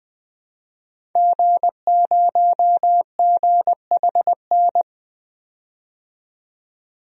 Morse code, heard as a keyed tone, about 20 wpm, 700 Hz. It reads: G0GHN
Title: DXCC-100 Call Signs - 5 Characters - 20wpm